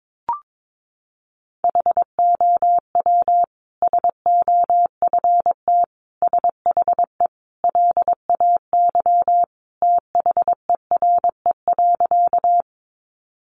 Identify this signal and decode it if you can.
Morse code, heard as a keyed tone, about 22 wpm, 700 Hz. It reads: HOW SOFT SHE LAY THERE.
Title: Sentences from Top 1500 Words 22wpm